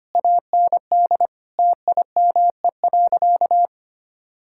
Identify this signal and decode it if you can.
Morse code, heard as a keyed tone, about 25 wpm, 700 Hz. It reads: AND TIME.
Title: Sentences from Top 1000 Words - Review Part 5 - 25wpm